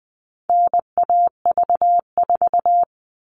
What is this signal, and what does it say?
Morse code, heard as a keyed tone, about 20 wpm, 700 Hz. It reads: NAV4
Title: All Cut Numbers - 4-Digit 20wpm